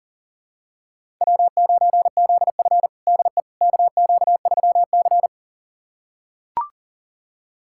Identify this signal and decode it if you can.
Morse code, heard as a keyed tone, about 40 wpm, 700 Hz. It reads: W9ZF DE KQ3C
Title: USA Callsign DE Callsign - 4 Characters - 40wpm